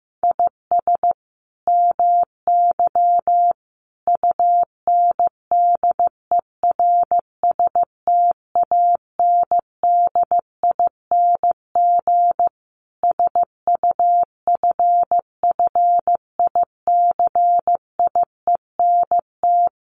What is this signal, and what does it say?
Morse code, heard as a keyed tone, about 14 wpm, 700 Hz. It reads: IS MY UNDERSTANDING SUFFICIENT
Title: Sentences from Top 1400 Words 15wpm